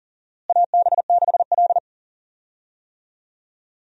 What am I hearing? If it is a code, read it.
Morse code, heard as a keyed tone, about 40 wpm, 700 Hz. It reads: AB6L
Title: USA Call Signs - 4 Characters - Repeated 3x - 40wpm